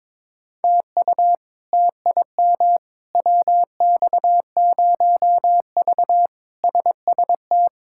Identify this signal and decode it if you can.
Morse code, heard as a keyed tone, about 21 wpm, 700 Hz. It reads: TU TIM WX0V SST
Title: K1USN SST QSO Practice - 22wpm